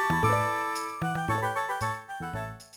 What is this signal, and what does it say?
There is no code.